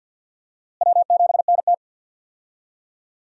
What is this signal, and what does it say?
Morse code, heard as a keyed tone, about 51 wpm, 700 Hz. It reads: W7NT
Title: USA Call Signs - 4 Characters - Repeated 3x - 50wpm